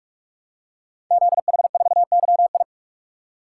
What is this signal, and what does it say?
Morse code, heard as a keyed tone, about 44 wpm, 700 Hz. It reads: ZH4YI
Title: Code Groups - 5 Letter:Numbers - 45wpm